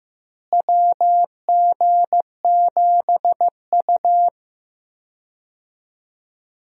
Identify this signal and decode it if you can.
Morse code, heard as a keyed tone, about 15 wpm, 700 Hz. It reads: WG7U
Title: USA Call Signs - 4 Characters - Repeated 3x - 15wpm